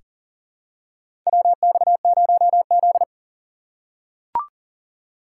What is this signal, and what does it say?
Morse code, heard as a keyed tone, about 40 wpm, 700 Hz. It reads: WX0Z
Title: K1USN SST QSO Practice - 40wpm